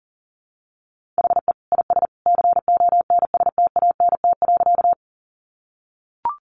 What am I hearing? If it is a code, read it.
Morse code, heard as a keyed tone, about 39 wpm, 700 Hz. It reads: HE IS CONSTANT.
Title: Sentences from Top 1500 Words 40wpm